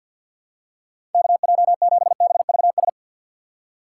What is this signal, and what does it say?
Morse code, heard as a keyed tone, about 52 wpm, 700 Hz. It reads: KJ7BVS
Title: USA Call Signs - 3x Character Spacing - 50wpm